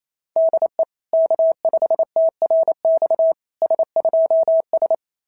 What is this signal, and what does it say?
Morse code, heard as a keyed tone, about 27 wpm, 650 Hz. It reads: DE K5TRX S2S
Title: SOTA QSO Practice 28wpm